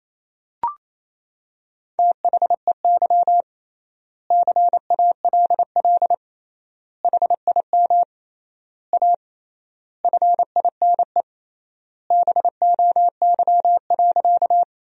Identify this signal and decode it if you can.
Morse code, heard as a keyed tone, about 28 wpm, 700 Hz. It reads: THEY CALL HIM A FINE BOY.
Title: 3x Word-Spacing - Sentences from Top 500 Words - 28wpm